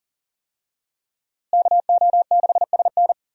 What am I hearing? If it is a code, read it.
Morse code, heard as a keyed tone, about 37 wpm, 700 Hz. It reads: KO6SN
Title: USA Call Signs - 5 Characters - Repeated 3x - 40wpm